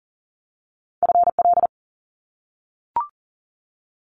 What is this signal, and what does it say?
Morse code, heard as a keyed tone, about 40 wpm, 700 Hz. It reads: FL
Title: US States Repeated 3x 40wpm